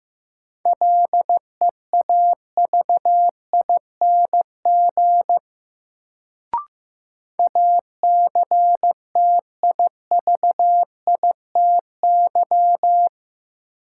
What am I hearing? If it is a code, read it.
Morse code, heard as a keyed tone, about 15 wpm, 700 Hz. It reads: LEAVING ACTIVITY
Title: Sentences from Top 1000 Words - Review Part 4 - New Word Drill - 15wpm